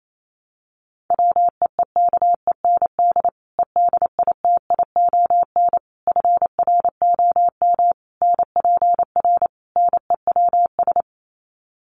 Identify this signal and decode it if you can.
Morse code, heard as a keyed tone, about 28 wpm, 700 Hz. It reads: WEEKEND EDITION FROM NPR NEWS
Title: Sets of 5 Words 28wpm